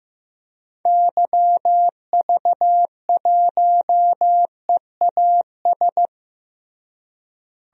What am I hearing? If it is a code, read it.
Morse code, heard as a keyed tone, about 15 wpm, 700 Hz. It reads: YV1EAS
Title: DXCC-100 Call Signs - 6 Characters - Repeated 2x - 15wpm